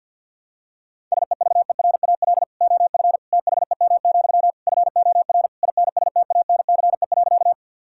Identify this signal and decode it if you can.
Morse code, heard as a keyed tone, about 47 wpm, 690 Hz. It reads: SEVERAL OF THEM, FOR INSTANCE.
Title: Sentences from Top 1000 Words - Review Part 5 - 50wpm